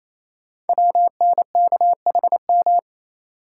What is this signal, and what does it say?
Morse code, heard as a keyed tone, about 29 wpm, 700 Hz. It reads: WNKHM
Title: Code Groups - 5 Letters - 28wpm